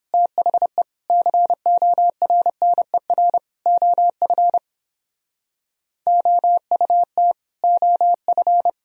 Morse code, heard as a keyed tone, about 31 wpm, 700 Hz. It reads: THE CORNER OF OUT OF